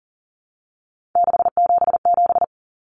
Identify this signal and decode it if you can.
Morse code, heard as a keyed tone, about 40 wpm, 700 Hz. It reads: 677
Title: Common Cut Numbers - 3-Digit 40wpm